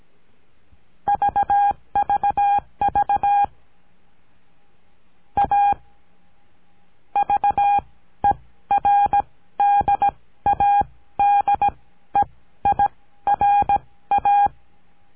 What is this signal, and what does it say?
Morse code, heard as a keyed tone, about 17 wpm, spaced as for 13 wpm, 810 Hz. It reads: VVV A VERDADEIRA
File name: CURSO DE TELEGRAFIA